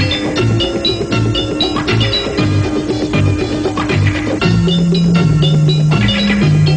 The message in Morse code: none